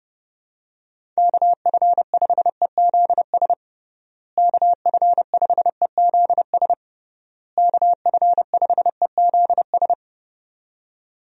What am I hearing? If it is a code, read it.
Morse code, heard as a keyed tone, about 30 wpm, 700 Hz. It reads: KF5EZS KF5EZS KF5EZS
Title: USA Call Signs - 6 Characters - Repeated 3x - 30wpm